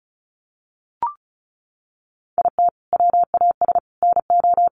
Morse code, heard as a keyed tone, about 34 wpm, 700 Hz. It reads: IT WAS NO
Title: Sentences from Top 1500 Words 35wpm